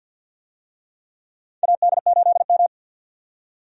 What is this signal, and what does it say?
Morse code, heard as a keyed tone, about 51 wpm, 700 Hz. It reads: AD8M
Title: USA Call Signs - 4 Characters - 50wpm